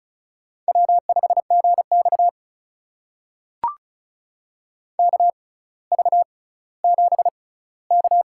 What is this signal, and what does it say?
Morse code, heard as a keyed tone, about 36 wpm, spaced as for 18 wpm, 700 Hz. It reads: W5GX KV7K
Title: USA Call Signs - 4 Characters - 6x Character Spacing - 35wpm